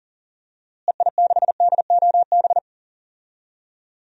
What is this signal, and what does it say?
Morse code, heard as a keyed tone, about 40 wpm, 700 Hz. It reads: EI6DOB
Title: DXCC-100 Call Signs - 6 Characters - Repeated 3x - 40wpm